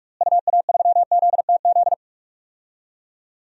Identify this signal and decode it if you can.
Morse code, heard as a keyed tone, about 42 wpm, 700 Hz. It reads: UA3ZTZ
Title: DXCC-100 Call Signs - 6 Characters - Repeated 2x - 45wpm